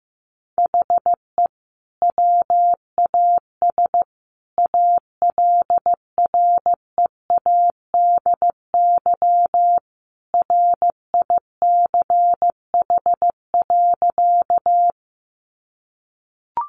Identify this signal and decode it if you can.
Morse code, heard as a keyed tone, about 15 wpm, 700 Hz. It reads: HE WAS ALREADY RICH.
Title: Sentences from Top 1100 Words 15wpm